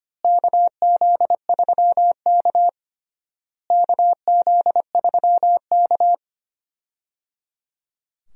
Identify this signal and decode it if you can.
Morse code, heard as a keyed tone, about 25 wpm, 700 Hz. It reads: KZ3K KZ3K
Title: USA Call Signs - 4 Characters - Repeated 3x - 25wpm